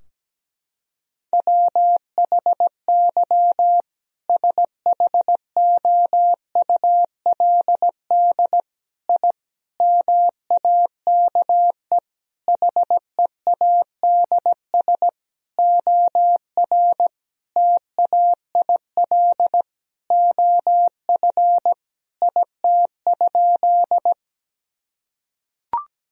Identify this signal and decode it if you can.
Morse code, heard as a keyed tone, about 17 wpm, 700 Hz. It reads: WHY SHOULD I MAKE HEADS OR TAIL OF IT?
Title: Sentences from Top 1000 Words - Review Part 3 - 17wpm